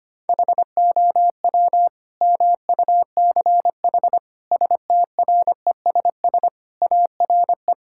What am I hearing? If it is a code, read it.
Morse code, heard as a keyed tone, about 25 wpm, 700 Hz. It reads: HOW MUCH STRESS ARE
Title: Sentences from Top 1000 Words - Review Part 6 - 25wpm